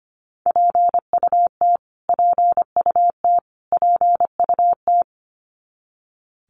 Morse code, heard as a keyed tone, about 25 wpm, 700 Hz. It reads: PUT PUT PUT
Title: Top 500 Words Repeated 25wpm